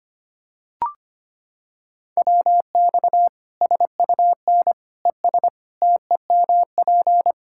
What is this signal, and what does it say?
Morse code, heard as a keyed tone, about 25 wpm, 700 Hz. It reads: WX SUN ES TEMP
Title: Practice QSO - Exchange 2 - Info - 25wpm